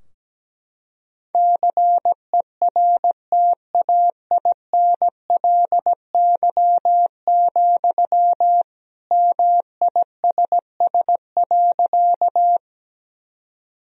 Morse code, heard as a keyed tone, about 17 wpm, 700 Hz. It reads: CERTAINLY, MISS.
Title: Sentences from Top 1000 Words - Review Part 3 - 17wpm